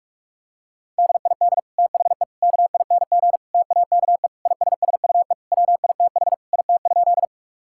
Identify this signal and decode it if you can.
Morse code, heard as a keyed tone, about 46 wpm, 700 Hz. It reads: DID THE KING TAKE ISSUE WITH IT?
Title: Sentences from Top 1000 Words - Review Part 6 - 45wpm